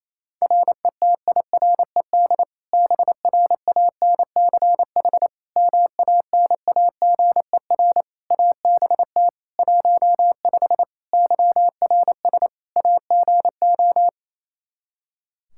Morse code, heard as a keyed tone, about 28 wpm, 700 Hz. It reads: RETIRED BRANCH MANAGER ABT 15 YRS AGO
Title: Practice QSO - Exchange 3 - More Info - 28wpm